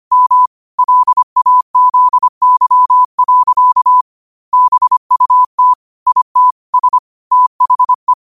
Morse code, heard as a keyed tone, about 25 wpm, 1000 Hz. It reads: M LAZY. BUT IT S THE